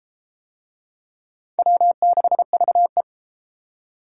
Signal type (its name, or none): Morse code